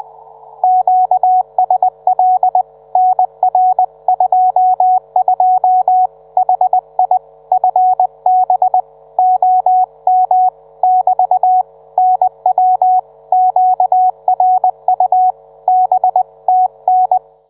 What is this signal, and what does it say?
Morse code, heard as a keyed tone, about 20 wpm, 740 Hz. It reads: QSL NR22 HI FB OM <BT> NW QRU B TN